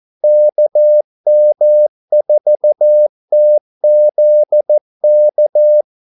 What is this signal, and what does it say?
Morse code, heard as a keyed tone, about 14 wpm, 600 Hz. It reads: KM4TZK
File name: Callsigns-14wpm